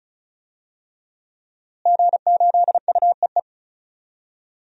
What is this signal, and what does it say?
Morse code, heard as a keyed tone, about 34 wpm, 700 Hz. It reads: G8UEE
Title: DXCC-100 Call Signs - 5 Characters - Repeated 3x - 35wpm